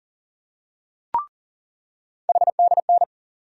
Morse code, heard as a keyed tone, about 40 wpm, 700 Hz. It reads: HDN
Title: USA Call Sign Postfixes - 3 Letters 40wpm